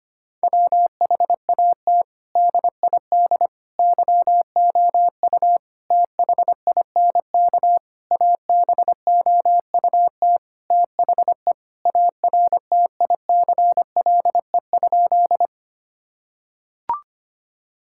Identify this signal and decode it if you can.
Morse code, heard as a keyed tone, about 25 wpm, 700 Hz. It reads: WHAT DID YOU THINK ABOUT THE ARTICLE?
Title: Sentences from Top 1000 Words - Review Part 5 - 25wpm